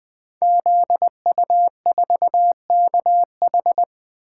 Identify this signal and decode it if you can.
Morse code, heard as a keyed tone, about 20 wpm, 700 Hz. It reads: ZU4KH
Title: Code Groups - 5 Letter:Numbers - 20wpm